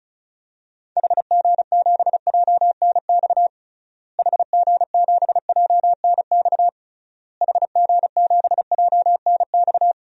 Morse code, heard as a keyed tone, about 36 wpm, 700 Hz. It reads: HG7JNX HG7JNX HG7JNX
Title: DXCC-100 Call Signs - 6 Characters - Repeated 3x - 35wpm